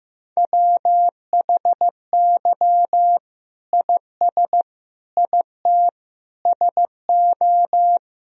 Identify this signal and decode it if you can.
Morse code, heard as a keyed tone, about 15 wpm, 700 Hz. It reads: WHY IS IT SO
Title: Sentences from Top 1000 Words - Review Part 1 - 15wpm